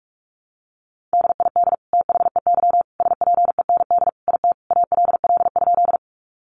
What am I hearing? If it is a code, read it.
Morse code, heard as a keyed tone, about 44 wpm, 700 Hz. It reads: DID THEY SPEND IT ALL?
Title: Sentences from Top 800 Words 45wpm